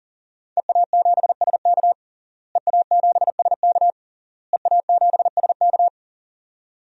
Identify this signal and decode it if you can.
Morse code, heard as a keyed tone, about 40 wpm, 700 Hz. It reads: EA7SK EA7SK EA7SK
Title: DXCC-100 Call Signs - 5 Characters - Repeated 3x - 40wpm